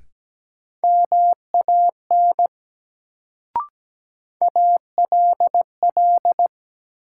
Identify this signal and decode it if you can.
Morse code, heard as a keyed tone, about 17 wpm, 700 Hz. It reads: MAN ALL
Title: Sets of 2 Words - New Word Drill - 17wpm